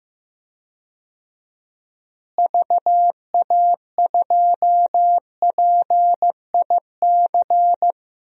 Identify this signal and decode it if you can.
Morse code, heard as a keyed tone, about 15 wpm, 700 Hz. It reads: VA2PIC
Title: DXCC-100 Call Signs - 6 Characters - Repeated 3x - 15wpm